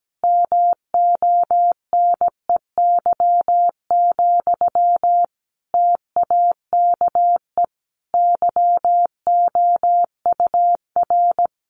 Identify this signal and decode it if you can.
Morse code, heard as a keyed tone, about 17 wpm, 700 Hz. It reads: MONEY, TAKE YOUR
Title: Sentences from Top 900 Words 17wpm